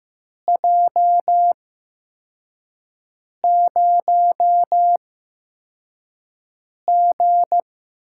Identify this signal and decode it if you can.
Morse code, heard as a keyed tone, about 15 wpm, spaced as for 4 wpm, 700 Hz. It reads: J0G
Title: USA Call Signs - 4 Characters - 8x Character Spacing - 15wpm